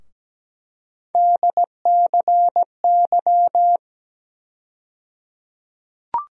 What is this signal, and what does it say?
Morse code, heard as a keyed tone, about 17 wpm, 700 Hz. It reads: DCY
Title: USA Call Sign Postfixes - 3 Letters 17wpm